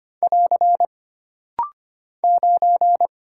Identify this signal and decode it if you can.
Morse code, heard as a keyed tone, about 25 wpm, 700 Hz. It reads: <AR> 9
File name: Single Character (Letter:Number:Punctuation:Prosign) - Rapid-Fire - 25wpm